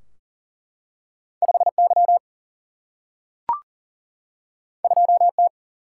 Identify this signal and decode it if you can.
Morse code, heard as a keyed tone, about 38 wpm, 700 Hz. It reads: 5Y 2T
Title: Single-Letter Single-Number 40wpm